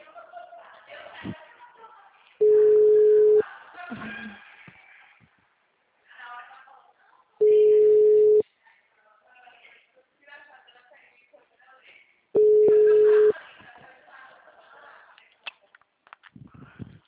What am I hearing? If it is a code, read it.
Ringback tone (Europe, 425 Hz), with speech.